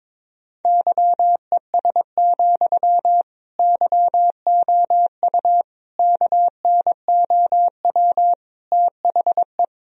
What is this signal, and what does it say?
Morse code, heard as a keyed tone, about 22 wpm, 700 Hz. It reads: YES, YOU KNOW THE
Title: Sentences from Top 1000 Words - Review Part 4 - 22wpm